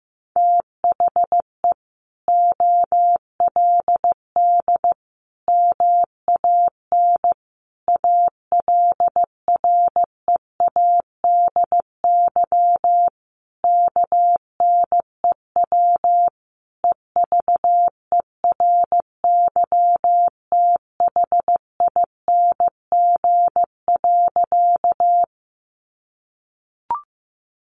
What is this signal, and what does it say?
Morse code, heard as a keyed tone, about 15 wpm, 700 Hz. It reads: THE OLD MAN ALREADY KNEW EVERYTHING.
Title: Sentences from Top 700 Words 15wpm